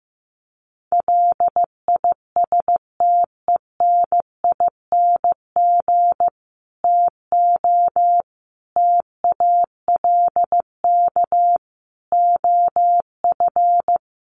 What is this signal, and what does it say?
Morse code, heard as a keyed tone, about 14 wpm, 700 Hz. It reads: LISTENING TO TALK OF
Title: Sets of 5 Words 15wpm